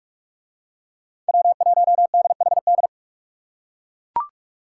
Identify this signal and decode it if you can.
Morse code, heard as a keyed tone, about 39 wpm, 700 Hz. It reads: W1DHD